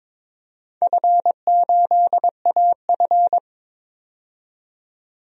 Morse code, heard as a keyed tone, about 22 wpm, 700 Hz. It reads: F8AF
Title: DXCC-100 Call Signs - 4 Characters - 22wpm